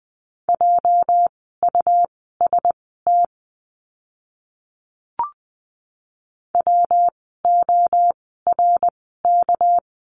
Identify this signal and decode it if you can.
Morse code, heard as a keyed tone, about 20 wpm, spaced as for 14 wpm, 700 Hz. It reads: JUST WORK
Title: 237 Lesson 38f - Letter q - Words 20wpm